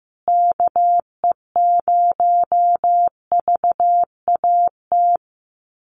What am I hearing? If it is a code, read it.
Morse code, heard as a keyed tone, about 15 wpm, 700 Hz. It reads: KE0VAT
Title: USA Call Signs Repeated 2x 15wpm